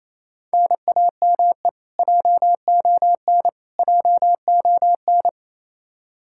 Morse code, heard as a keyed tone, about 28 wpm, 700 Hz. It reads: NAME JON JON
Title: K1USN SST QSO Practice 28wpm